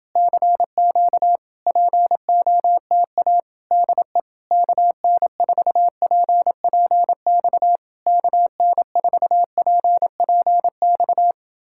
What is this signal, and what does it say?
Morse code, heard as a keyed tone, about 28 wpm, 700 Hz. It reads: CQ POTA DE KN4PPX KN4PPX